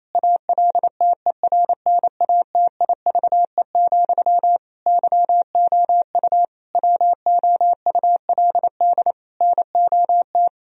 Morse code, heard as a keyed tone, about 28 wpm, 700 Hz. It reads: ALTERNATIVE, YOU WOULD NOT
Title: Sentences from Top 1000 Words - Review Part 4 - 28wpm